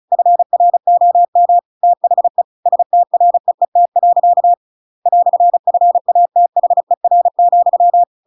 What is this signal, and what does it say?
Morse code, heard as a keyed tone, about 34 wpm, 700 Hz. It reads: FROM THE STREET. "FATHER,
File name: Pinocchio / Pinocchio ch7 35 35 ik1hge